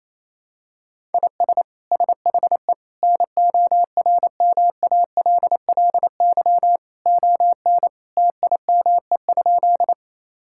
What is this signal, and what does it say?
Morse code, heard as a keyed tone, about 28 wpm, 700 Hz. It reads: IS SHE NORMALLY ON TIME?
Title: Sentences from Top 1000 Words - Review Part 6 - 28wpm